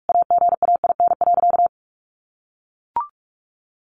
Morse code, heard as a keyed tone, about 44 wpm, 700 Hz. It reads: AGAIN.
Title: Sentences from Top 1100 Words 45wpm